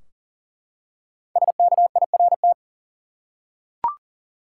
Morse code, heard as a keyed tone, about 40 wpm, 700 Hz. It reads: SKIRT
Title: Five Letter Words - 40wpm